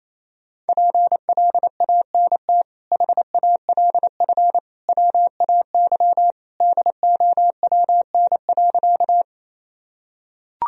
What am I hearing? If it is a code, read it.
Morse code, heard as a keyed tone, about 28 wpm, 700 Hz. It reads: PLANT HALF WAY DOWN.
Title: Sentences from Top 1000 Words - Review Part 1 - 28wpm